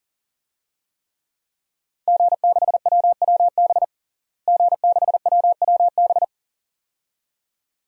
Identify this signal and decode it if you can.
Morse code, heard as a keyed tone, about 40 wpm, 700 Hz. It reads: G6WWB G6WWB
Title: DXCC-100 Call Signs - 5 Characters - Repeated 2x - 40wpm